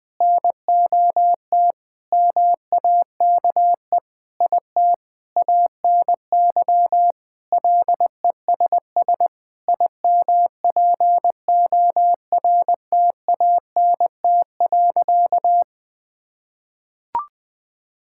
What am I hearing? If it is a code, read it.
Morse code, heard as a keyed tone, about 20 wpm, 700 Hz. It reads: NOT MAKE IT ANY LESS IMPORTANT.
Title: Sentences with 7-Word Phrases - 20wpm